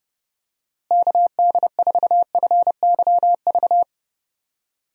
Morse code, heard as a keyed tone, about 30 wpm, 700 Hz. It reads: KD4FYV
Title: USA Call Signs - 6 Characters - Repeated 3x - 30wpm